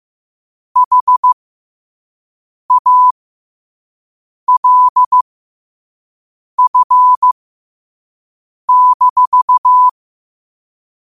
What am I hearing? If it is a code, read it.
Morse code, heard as a keyed tone, about 15 wpm, spaced as for 5 wpm, 1000 Hz. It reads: HALF-
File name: qotdfiles / 5wpm